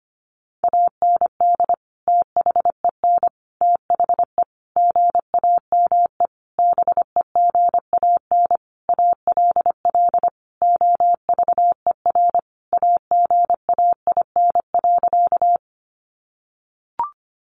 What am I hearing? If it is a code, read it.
Morse code, heard as a keyed tone, about 25 wpm, 700 Hz. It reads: AND THEN THE GAME BEGAN ALL OVER AGAIN.
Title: Sentences from Top 500 Words 25wpm